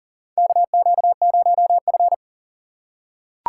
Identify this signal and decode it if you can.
Morse code, heard as a keyed tone, about 40 wpm, 700 Hz. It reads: KQ0F
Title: USA Call Signs - 4 Characters - 4x Character Spacing - 40wpm